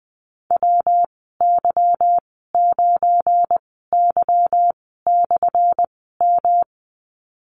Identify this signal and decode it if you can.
Morse code, heard as a keyed tone, about 20 wpm, spaced as for 15 wpm, 700 Hz. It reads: WY9Y/M
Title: 179 Lesson 29g - Slash - Call Signs 20wpm